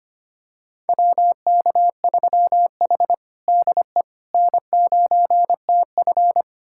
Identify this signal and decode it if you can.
Morse code, heard as a keyed tone, about 25 wpm, 700 Hz. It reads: WK3H DE N9TF
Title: USA Callsign DE Callsign - 4 Characters - 25wpm